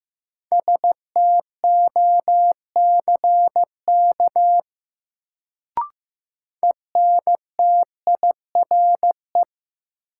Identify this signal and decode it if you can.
Morse code, heard as a keyed tone, about 16 wpm, 700 Hz. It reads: STOCK ENTIRE
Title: Sentences from Top 800 Words - New Word Drill - 15wpm